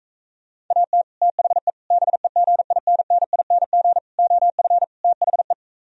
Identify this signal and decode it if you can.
Morse code, heard as a keyed tone, about 46 wpm, 700 Hz. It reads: AT THE BEGINNING OF THE